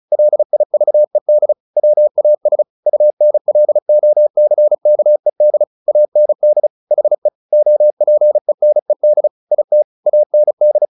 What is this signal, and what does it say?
Morse code, heard as a keyed tone, about 34 wpm, 600 Hz. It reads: LIVED WAS UNLOCKED AND HE OPENED IT AND
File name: the old man and the sea / the old man and the sea 1 35wpm